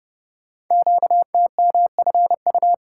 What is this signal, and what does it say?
Morse code, heard as a keyed tone, about 29 wpm, 690 Hz. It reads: QTMFU
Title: Code Groups - 5 Letters - 30wpm